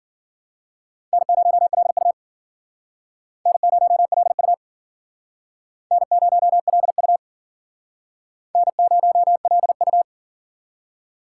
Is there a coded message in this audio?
Yes: Morse code from 1.1 to 10.0 s, heard as a keyed tone, about 54 wpm, 700 Hz. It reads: N0LU N0LU N0LU N0LU